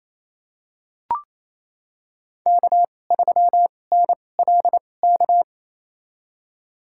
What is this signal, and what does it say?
Morse code, heard as a keyed tone, about 28 wpm, spaced as for 20 wpm, 700 Hz. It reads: K3NLK
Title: USA Call Signs - 5 Characters - 2x Character Spacing - 28wpm